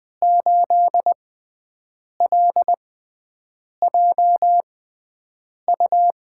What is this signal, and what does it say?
Morse code, heard as a keyed tone, about 20 wpm, spaced as for 7 wpm, 700 Hz. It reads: 8LJU
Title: USA Call Signs - 6 Characters - 6x Character Spacing - 20wpm